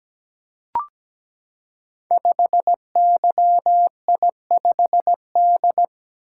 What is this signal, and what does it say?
Morse code, heard as a keyed tone, about 17 wpm, 700 Hz. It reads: 5YI5D
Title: Code Groups - 5 Letter:Numbers - 17wpm